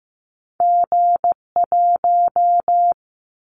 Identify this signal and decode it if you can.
Morse code, heard as a keyed tone, about 15 wpm, 700 Hz. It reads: G1
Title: Callsign Prefixes for Top 25 DXCC - 15wpm